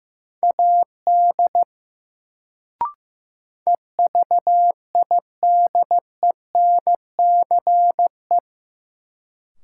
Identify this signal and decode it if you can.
Morse code, heard as a keyed tone, about 15 wpm, 700 Hz. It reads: AD EVIDENCE
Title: Sentences from Top 1300 Words - New Word Drill - 15wpm